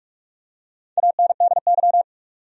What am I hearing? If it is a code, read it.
Morse code, heard as a keyed tone, about 46 wpm, 700 Hz. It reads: ANDY
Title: USA Common Names - 45wpm